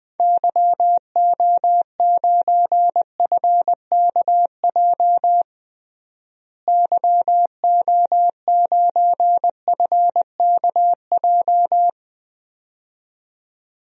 Morse code, heard as a keyed tone, about 20 wpm, 700 Hz. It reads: YO9FKJ YO9FKJ
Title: DXCC-100 Call Signs - 6 Characters - Repeated 2x - 20wpm